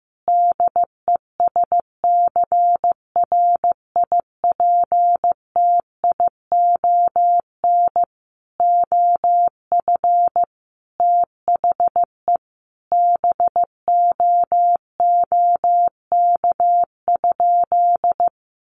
Morse code, heard as a keyed tone, about 15 wpm, 700 Hz. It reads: DESCRIPTION OF THE BOOK?
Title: Sentences from Top 1300 Words 15wpm